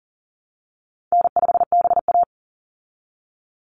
Morse code, heard as a keyed tone, about 40 wpm, 700 Hz. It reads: N5BA
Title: Call Signs Repeated 3x - 40wpm